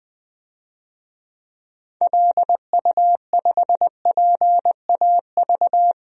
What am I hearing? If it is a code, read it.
Morse code, heard as a keyed tone, about 20 wpm, 700 Hz. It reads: LU5PAV
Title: DXCC-100 Call Signs - 6 Characters - Repeated 2x - 20wpm